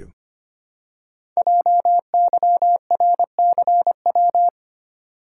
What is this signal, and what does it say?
Morse code, heard as a keyed tone, about 25 wpm, 700 Hz. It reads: JYRCW
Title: Code Groups - 5 Letter:Numbers - 25wpm